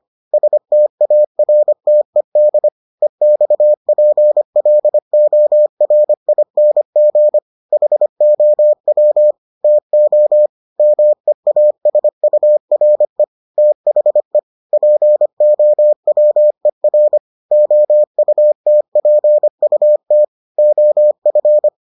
Morse code, heard as a keyed tone, about 25 wpm, 600 Hz. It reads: STARTED EXPLORING HOW TO MEASURE THE POWER OUTPUT OF